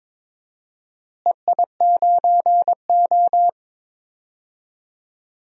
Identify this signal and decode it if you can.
Morse code, heard as a keyed tone, about 22 wpm, 700 Hz. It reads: EI9O
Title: DXCC-100 Call Signs - 4 Characters - Repeated 3x - 22wpm